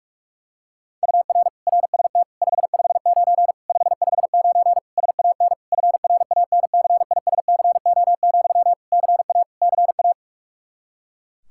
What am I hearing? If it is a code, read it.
Morse code, heard as a keyed tone, about 46 wpm, 700 Hz. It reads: UR RST 559 559 SAN FRANCISCO, CA CA
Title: Practice QSO - Exchange 1 - Beginning - 45wpm